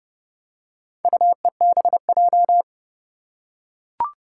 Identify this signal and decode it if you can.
Morse code, heard as a keyed tone, about 31 wpm, 700 Hz. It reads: UEBJ
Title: Four Challenging Letter-Number Combinations 30wpm